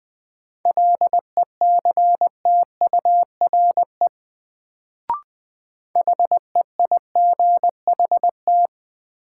Morse code, heard as a keyed tone, about 21 wpm, 700 Hz. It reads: LECTURE HEIGHT
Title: Sentences from Top 1600 Words - New Word Drill - 20wpm